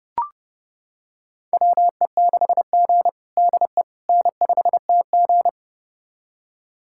Morse code, heard as a keyed tone, about 31 wpm, 700 Hz. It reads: WE6G DE N5TG
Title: USA Callsign DE Callsign - 4 Characters - 30wpm